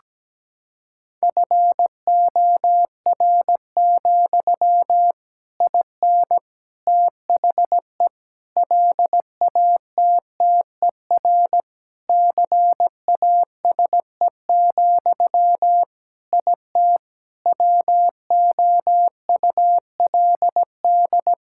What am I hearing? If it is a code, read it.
Morse code, heard as a keyed tone, about 17 wpm, 700 Hz. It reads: FOR, IN THE LATTER CASE, IT WOULD
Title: Sentences from Top 1600 Words 17wpm